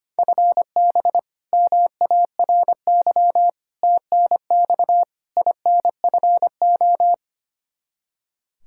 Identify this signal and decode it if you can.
Morse code, heard as a keyed tone, about 25 wpm, 700 Hz. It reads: FB MARY TNX INFO
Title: Practice QSO - Exchange 3 - More Info - 25wpm